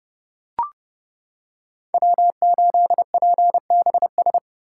Morse code, heard as a keyed tone, about 29 wpm, 700 Hz. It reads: W8PBS